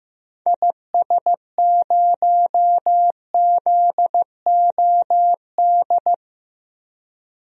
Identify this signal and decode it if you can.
Morse code, heard as a keyed tone, about 15 wpm, 700 Hz. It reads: IS0ZOD
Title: DXCC-100 Call Signs - 6 Characters - 15wpm